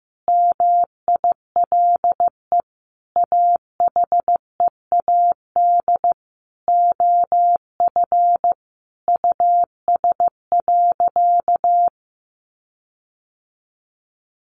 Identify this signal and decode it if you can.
Morse code, heard as a keyed tone, about 15 wpm, 700 Hz. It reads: MILE AHEAD OF US.
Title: Sentences from Top 1300 Words 15wpm